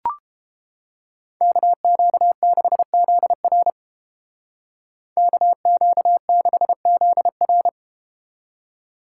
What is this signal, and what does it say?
Morse code, heard as a keyed tone, about 31 wpm, 700 Hz. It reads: KQ6ZR KQ6ZR